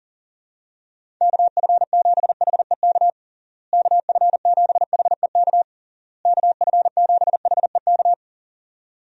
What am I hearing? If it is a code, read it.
Morse code, heard as a keyed tone, about 40 wpm, 700 Hz. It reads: KF7HEK KF7HEK KF7HEK
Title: USA Call Signs - 6 Characters - Repeated 3x - 40wpm